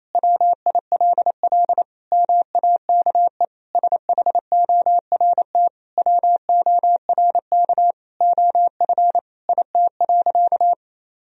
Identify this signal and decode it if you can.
Morse code, heard as a keyed tone, about 28 wpm, 700 Hz. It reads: WILL MAKE SHORT WORK OF IT.
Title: Sentences from Top 1000 Words - Review Part 2 - 28wpm